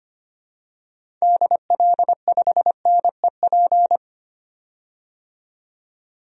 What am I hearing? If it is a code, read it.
Morse code, heard as a keyed tone, about 25 wpm, 700 Hz. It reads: DL5NEP
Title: DXCC-100 Call Signs - 6 Characters - 25wpm